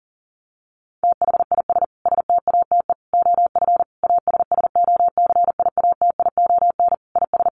Morse code, heard as a keyed tone, about 40 wpm, 700 Hz. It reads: THIS STATE OF ASSOCIATION IS
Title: Sentences from Top 1100 Words 40wpm